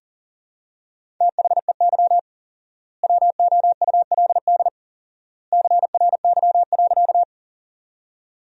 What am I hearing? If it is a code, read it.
Morse code, heard as a keyed tone, about 37 wpm, 700 Hz. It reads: THEY WOULD CRY.
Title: 4x Word-Spacing - Sentences from Top 500 Words - 40wpm